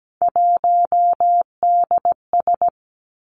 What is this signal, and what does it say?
Morse code, heard as a keyed tone, about 17 wpm, 700 Hz. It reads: 1DS
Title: Three Challenging Letter-Number Combinations 17wpm